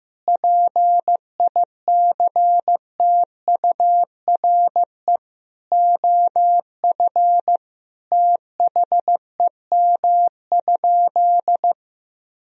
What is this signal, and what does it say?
Morse code, heard as a keyed tone, about 15 wpm, 700 Hz. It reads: PICTURE OF THEM?
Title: Sentences from Top 1000 Words - Review Part 2 - 15wpm